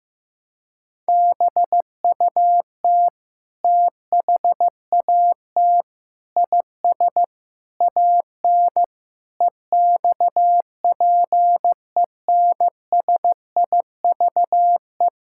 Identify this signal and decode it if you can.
Morse code, heard as a keyed tone, about 15 wpm, 700 Hz. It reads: BUT THAT IS AN EXPENSIVE
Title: Sentences from Top 1000 Words - Review Part 6 - 15wpm